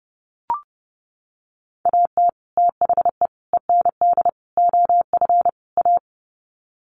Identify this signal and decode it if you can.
Morse code, heard as a keyed tone, about 30 wpm, 700 Hz. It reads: AT THE END OF A
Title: Sets of 5 Words 30wpm